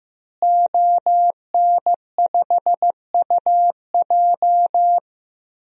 Morse code, heard as a keyed tone, about 15 wpm, 700 Hz. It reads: ON5UJ
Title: DXCC-100 Call Signs - 5 Characters - Repeated 2x - 15wpm